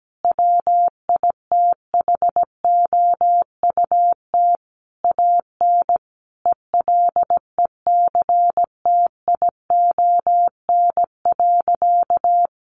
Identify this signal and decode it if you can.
Morse code, heard as a keyed tone, about 17 wpm, 700 Hz. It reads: WITHOUT AN ELECTION.
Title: Sentences from Top 1500 Words 17wpm